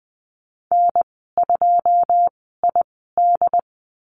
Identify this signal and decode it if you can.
Morse code, heard as a keyed tone, about 20 wpm, spaced as for 15 wpm, 700 Hz. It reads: N2ID
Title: 213 Lesson 34g - Number 8 - Call Signs 20wpm